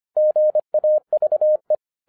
Morse code, heard as a keyed tone, about 25 wpm, 610 Hz. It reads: GAVE